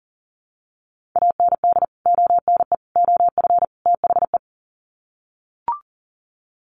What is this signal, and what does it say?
Morse code, heard as a keyed tone, about 40 wpm, 700 Hz. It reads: AND ONE OF THE
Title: Sets of 4 Words 40wpm